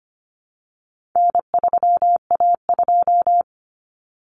Morse code, heard as a keyed tone, about 25 wpm, 700 Hz. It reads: N3A2
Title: Common Cut Numbers - 4-Digit 25wpm